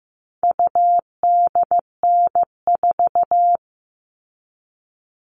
Morse code, heard as a keyed tone, about 15 wpm, 700 Hz. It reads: UDN4
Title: All Cut Numbers - 4-Digit 15wpm